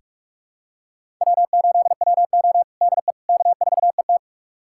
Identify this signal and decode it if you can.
Morse code, heard as a keyed tone, about 46 wpm, 700 Hz. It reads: W8WO DE K4ET
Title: USA Callsign DE Callsign - 4 Characters - 45wpm